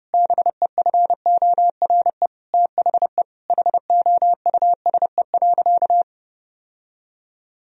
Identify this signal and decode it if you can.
Morse code, heard as a keyed tone, about 30 wpm, 700 Hz. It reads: BEFORE THE HOUSE.
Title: Sentences from Top 1000 Words - Review Part 2 - 30wpm